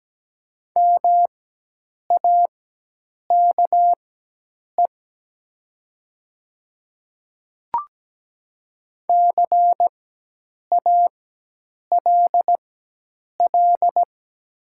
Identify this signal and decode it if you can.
Morse code, heard as a keyed tone, about 17 wpm, spaced as for 8 wpm, 700 Hz. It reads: MAKE CALL
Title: 4x Character Spacing - Top 100 Words - 17wpm